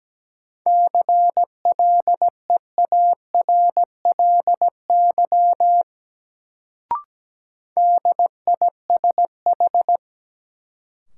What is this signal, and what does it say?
Morse code, heard as a keyed tone, about 17 wpm, 700 Hz. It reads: CLEARLY DISH
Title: Sentences from Top 1300 Words - New Word Drill Alt - 17wpm